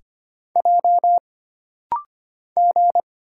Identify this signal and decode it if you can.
Morse code, heard as a keyed tone, about 25 wpm, 700 Hz. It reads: J G
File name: Single Character (Letter:Number:Punctuation:Prosign) - Rapid-Fire - 25wpm